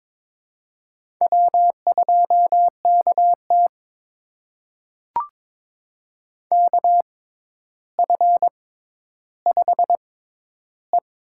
Morse code, heard as a keyed tone, about 22 wpm, spaced as for 11 wpm, 700 Hz. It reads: W2KT KF5E
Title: USA Call Signs - 4 Characters - 6x Character Spacing - 22wpm